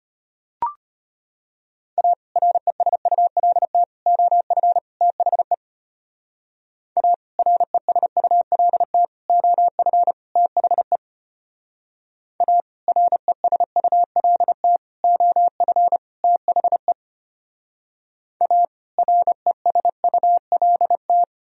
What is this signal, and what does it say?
Morse code, heard as a keyed tone, about 31 wpm, 700 Hz. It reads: A RESULT OF THE A RESULT OF THE A RESULT OF THE A RESULT
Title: Speed-Racing - Sets of 4 Words 25wpm